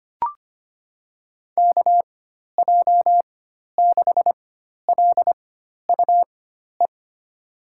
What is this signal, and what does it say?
Morse code, heard as a keyed tone, about 25 wpm, spaced as for 12 wpm, 700 Hz. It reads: KJ6LUE
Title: USA Call Signs - 4x Character Spacing - 25wpm